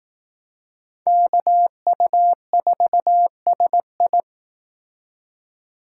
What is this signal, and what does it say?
Morse code, heard as a keyed tone, about 18 wpm, 700 Hz. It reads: KU4SI
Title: Speed Racing - USA 2x2 Call Signs 15wpm